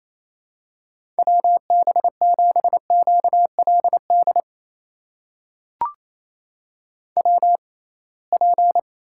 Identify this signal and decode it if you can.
Morse code, heard as a keyed tone, about 28 wpm, spaced as for 21 wpm, 700 Hz. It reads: WB7QLD WP
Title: USA Call Signs - 6 Characters - 6x Character Spacing - 28wpm